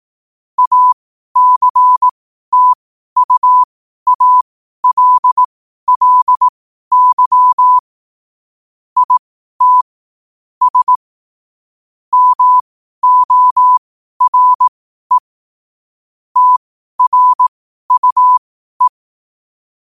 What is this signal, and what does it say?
Morse code, heard as a keyed tone, about 18 wpm, spaced as for 12 wpm, 1000 Hz. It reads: ACTUALLY IT S MORE TRUE
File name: qotdfiles / 12wpm